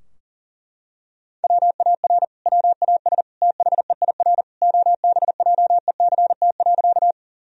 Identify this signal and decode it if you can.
Morse code, heard as a keyed tone, about 40 wpm, 700 Hz. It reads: WAR WAS THEIR OBJECT.
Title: Sentences from Top 1000 Words - Review Part 3 - 40wpm